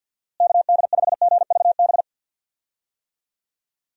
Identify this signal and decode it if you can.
Morse code, heard as a keyed tone, about 52 wpm, 700 Hz. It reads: KD5GVB
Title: USA Call Signs - 6 Characters - Repeated 2x - 50wpm